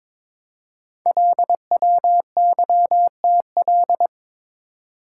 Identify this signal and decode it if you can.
Morse code, heard as a keyed tone, about 22 wpm, 700 Hz. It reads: LWYTL
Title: Code Groups - 5 Letters - 22wpm